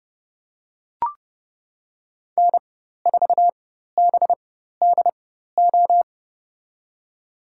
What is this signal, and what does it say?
Morse code, heard as a keyed tone, about 30 wpm, spaced as for 14 wpm, 700 Hz. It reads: N4BDO
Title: USA Call Signs - 4x Character Spacing - 30wpm